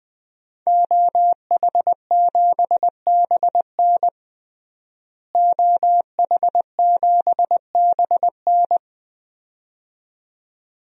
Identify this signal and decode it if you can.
Morse code, heard as a keyed tone, about 20 wpm, 700 Hz. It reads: OH7BN OH7BN
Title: DXCC-100 Call Signs - 5 Characters - Repeated 2x - 20wpm